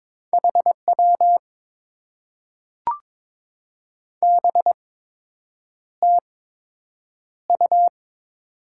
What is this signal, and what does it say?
Morse code, heard as a keyed tone, about 22 wpm, spaced as for 8 wpm, 690 Hz. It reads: HW BTU
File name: Top 90 QSO Elements - 8x Character Spacing - 22wpm